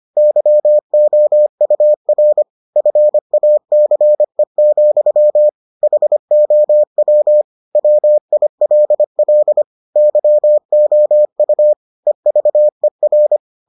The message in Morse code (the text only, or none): YOUR FACE, HOW WILL YOU EVER